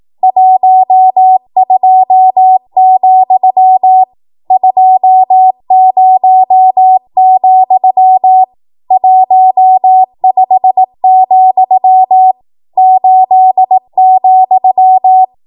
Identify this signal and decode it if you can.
Morse code, heard as a keyed tone, about 18 wpm, 750 Hz. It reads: 12, 20, 15, 8,
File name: Morse / 18WPM